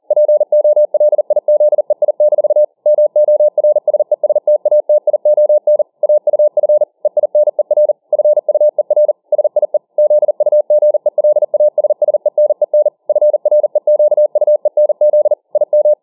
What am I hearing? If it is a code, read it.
Morse code, heard as a keyed tone, about 40 wpm, 600 Hz. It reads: POLIZEI- MORSESTATION AUF EINER FUER SIE ZUGELASSENEN FREQUENZ IM